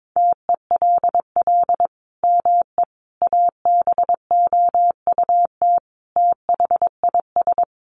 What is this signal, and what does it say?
Morse code, heard as a keyed tone, about 22 wpm, 700 Hz. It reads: TELL ME ABOUT THIS
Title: Sentences from Top 1400 Words 22wpm